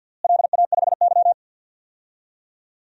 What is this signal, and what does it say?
Morse code, heard as a keyed tone, about 49 wpm, 700 Hz. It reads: LA5Y
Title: DXCC-100 Call Signs - 4 Characters - Repeated 2x - 50wpm